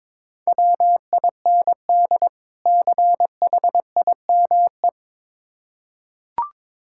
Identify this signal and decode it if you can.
Morse code, heard as a keyed tone, about 22 wpm, 690 Hz. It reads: WIND CHIME
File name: Difficult Words with Hints - 22wpm